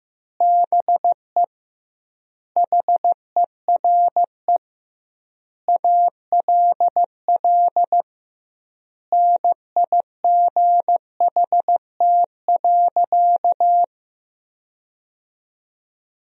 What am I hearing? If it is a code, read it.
Morse code, heard as a keyed tone, about 15 wpm, 700 Hz. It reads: BE HERE ALL NIGHT.
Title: 2x Word-Spacing - Sentences from Top 300 Words - 15wpm